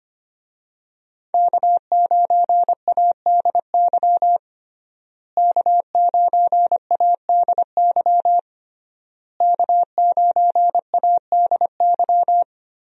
Morse code, heard as a keyed tone, about 25 wpm, 700 Hz. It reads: K9ADY K9ADY K9ADY
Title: USA Call Signs - 5 Characters - Repeated 3x - 25wpm